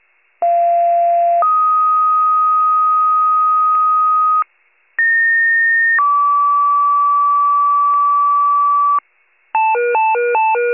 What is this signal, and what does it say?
Two-tone page, heard as a one-second tone, then a three-second tone, about 690 then 1250 Hz, about 1850 then 1150 Hz. It is heard from 0.4 to 9.0 s.